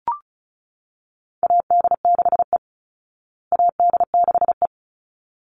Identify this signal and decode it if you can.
Morse code, heard as a keyed tone, about 34 wpm, 700 Hz. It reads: AD6E AD6E
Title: Call Signs Repeated 3x - 35wpm